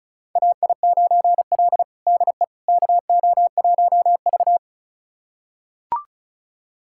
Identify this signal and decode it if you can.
Morse code, heard as a keyed tone, about 34 wpm, 700 Hz. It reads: AI9L DE KO1V
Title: USA Callsign DE Callsign - 4 Characters - 35wpm